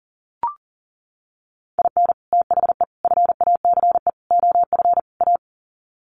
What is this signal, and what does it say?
Morse code, heard as a keyed tone, about 40 wpm, 700 Hz. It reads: IN THE FACE OF A
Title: Sets of 5 Words Encore 40wpm